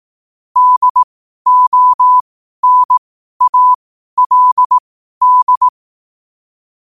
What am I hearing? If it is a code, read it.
Morse code, heard as a keyed tone, about 18 wpm, spaced as for 12 wpm, 1000 Hz. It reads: DONALD
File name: qotdfiles / 12wpm